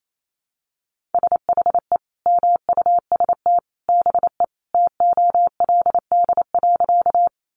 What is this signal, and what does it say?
Morse code, heard as a keyed tone, about 28 wpm, 700 Hz. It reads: SHE MUST BE TOLD.
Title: Sentences from Top 400 Words 28wpm